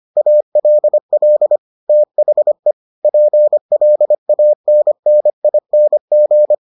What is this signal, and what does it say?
Morse code, heard as a keyed tone, about 25 wpm, 600 Hz. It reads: ALL THE PLANNING